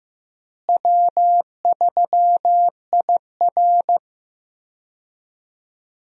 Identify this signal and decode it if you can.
Morse code, heard as a keyed tone, about 15 wpm, 700 Hz. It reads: W3IR
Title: USA Call Signs - 4 Characters - Repeated 3x - 15wpm